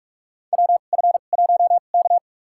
Morse code, heard as a keyed tone, about 47 wpm, spaced as for 33 wpm, 700 Hz. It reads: WF1K
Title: USA Call Signs - 4 Characters - 2x Character Spacing - 45wpm